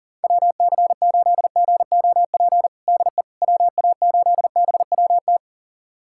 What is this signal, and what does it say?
Morse code, heard as a keyed tone, about 40 wpm, 700 Hz. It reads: WC8GOP DE WA8BWT
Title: USA Callsign DE Callsign 40wpm